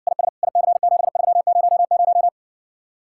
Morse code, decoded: IS IC7300

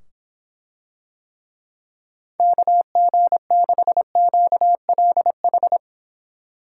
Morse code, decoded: KG6QLH

